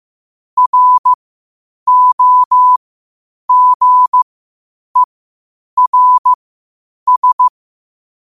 Morse code, heard as a keyed tone, about 15 wpm, spaced as for 8 wpm, 1000 Hz. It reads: ROGERS
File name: qotdfiles / 8wpm